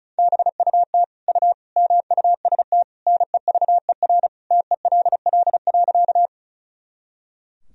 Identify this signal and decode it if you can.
Morse code, heard as a keyed tone, about 36 wpm, 700 Hz. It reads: BUT U MUST NEVER TELL.
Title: Sentences from Top 300 Words and Abbreviations 35wpm